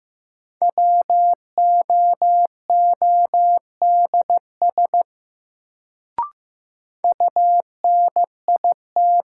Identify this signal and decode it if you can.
Morse code, heard as a keyed tone, about 16 wpm, 700 Hz. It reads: WOODS UNIT
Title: Sentences from Top 1000 Words - Review Part 2 - New Word Drill - 15wpm